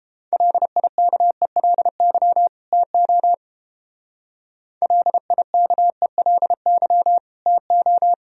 Morse code, heard as a keyed tone, about 30 wpm, 700 Hz. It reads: LIKELY TO LIKELY TO